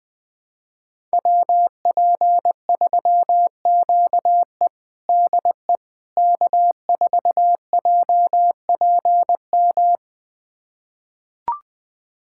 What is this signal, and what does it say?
Morse code, heard as a keyed tone, about 21 wpm, 700 Hz. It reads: WP3QE DE K4JPM
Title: USA Callsign DE Callsign - 5 Characters - 20wpm